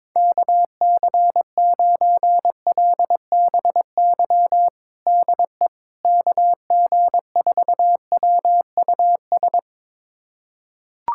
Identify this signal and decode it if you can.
Morse code, heard as a keyed tone, about 21 wpm, 700 Hz. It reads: KC9LBY DE KG4WUS
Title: USA Callsign DE Callsign - 6 Characters - 22wpm